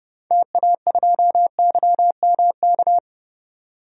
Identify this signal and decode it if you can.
Morse code, heard as a keyed tone, about 30 wpm, 700 Hz. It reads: TA2YMK
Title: DXCC-100 Call Signs - 6 Characters - 30wpm